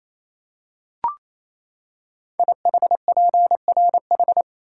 Morse code, heard as a keyed tone, about 27 wpm, 700 Hz. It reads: IHPRH
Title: Code Groups - 5 Letters - 28wpm